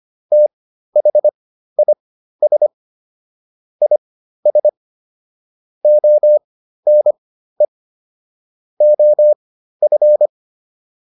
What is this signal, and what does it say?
Morse code, heard as a keyed tone, about 25 wpm, spaced as for 13 wpm, 600 Hz. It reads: THIS IS ONE OF